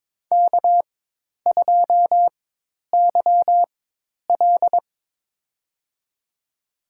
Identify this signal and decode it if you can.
Morse code, heard as a keyed tone, about 22 wpm, spaced as for 10 wpm, 700 Hz. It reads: K2YL
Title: USA Call Signs - 4x Character Spacing - 22wpm